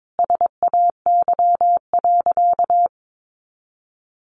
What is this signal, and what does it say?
Morse code, heard as a keyed tone, about 22 wpm, 700 Hz. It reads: SAY.
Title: Sentences from Top 1200 Words 22wpm